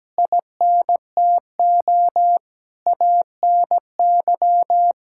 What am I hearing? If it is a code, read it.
Morse code, heard as a keyed tone, about 17 wpm, 700 Hz. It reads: INTO ANY
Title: Sentences from Top 1000 Words - Review Part 5 - 17wpm